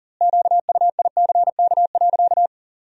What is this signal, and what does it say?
Morse code, heard as a keyed tone, about 40 wpm, 700 Hz. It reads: QUICK.
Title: 2x Word-Spacing - Sentences from Top 500 Words - 40wpm